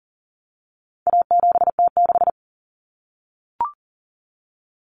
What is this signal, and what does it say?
Morse code, heard as a keyed tone, about 40 wpm, 700 Hz. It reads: A7T6
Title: Common Cut Numbers - 4-Digit 40wpm